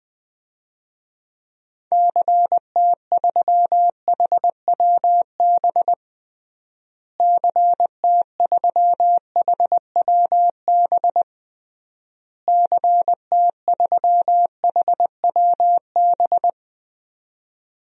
Morse code, heard as a keyed tone, about 20 wpm, 700 Hz. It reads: CT3HWB CT3HWB CT3HWB
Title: DXCC-100 Call Signs - 6 Characters - Repeated 3x - 20wpm